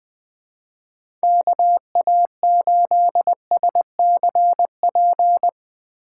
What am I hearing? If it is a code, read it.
Morse code, heard as a keyed tone, about 20 wpm, 700 Hz. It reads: KA8SCP
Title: Speed Racing - USA 2x3 Call Signs 17wpm